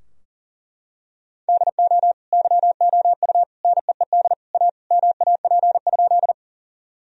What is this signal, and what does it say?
Morse code, heard as a keyed tone, about 40 wpm, 700 Hz. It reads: DO YOU NEED A MAP?
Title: Sentences from Top 1000 Words - Review Part 2 - 40wpm